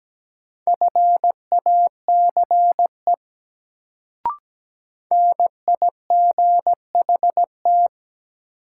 Morse code, heard as a keyed tone, about 17 wpm, 700 Hz. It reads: FACE NIGHT
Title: Sentences from Top 300 Words - New Word Drill - 17wpm